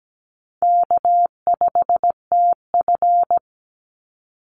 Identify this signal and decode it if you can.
Morse code, heard as a keyed tone, about 17 wpm, 700 Hz. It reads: K5TF
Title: Call Signs Repeated 2x - 17wpm